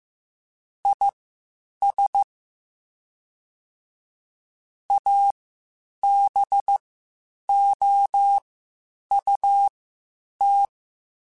Morse code, heard as a keyed tone, about 15 wpm, spaced as for 8 wpm, 800 Hz. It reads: IS ABOUT
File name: Spaced Characters / Spaced Characters-mp3s / Weather 4 21 sp @13